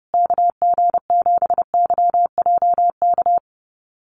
Morse code, heard as a keyed tone, about 30 wpm, 700 Hz. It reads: KG7YJK
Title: USA Call Signs Repeated 3x 30wpm